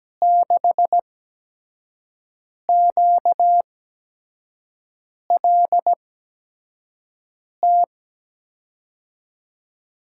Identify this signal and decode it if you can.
Morse code, heard as a keyed tone, about 17 wpm, 700 Hz. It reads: 6 Q L T
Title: USA Call Signs - 6 Characters - 8x Character Spacing - 17wpm